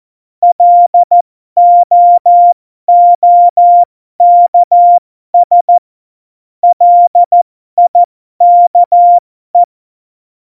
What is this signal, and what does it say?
Morse code, heard as a keyed tone, about 14 wpm, spaced as for 12 wpm, 700 Hz. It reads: LOOKS LIKE
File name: Morsekurs / Witze